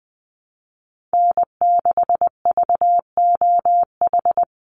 Morse code, heard as a keyed tone, about 20 wpm, 700 Hz. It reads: N6VOH
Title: Call Signs - 20wpm